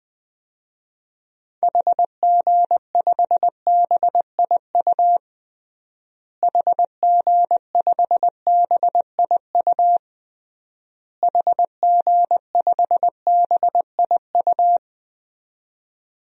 Morse code, heard as a keyed tone, about 21 wpm, 700 Hz. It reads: HG5BIU HG5BIU HG5BIU
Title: DXCC-100 Call Signs - 6 Characters - Repeated 3x - 20wpm